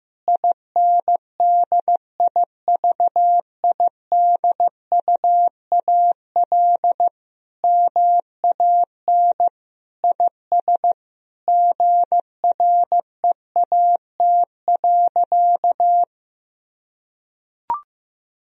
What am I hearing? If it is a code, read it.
Morse code, heard as a keyed tone, about 15 wpm, 700 Hz. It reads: INDIVIDUAL MAN IS GREAT.
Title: Sentences from Top 1000 Words - Review Part 4 - 15wpm